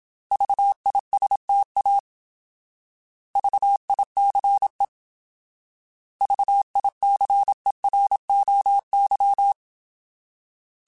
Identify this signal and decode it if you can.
Morse code, heard as a keyed tone, about 27 wpm, 800 Hz. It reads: VISTA VICE VICEROY